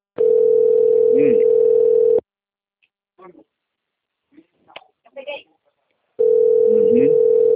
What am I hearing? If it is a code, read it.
Ringback tone (North America), 440 + 480 Hz, with speech.
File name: rg-1600-unknown- - - .49532